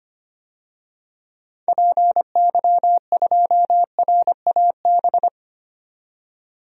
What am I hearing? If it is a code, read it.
Morse code, heard as a keyed tone, about 25 wpm, 700 Hz. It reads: PY2RAB
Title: DXCC-100 Call Signs - 6 Characters - 25wpm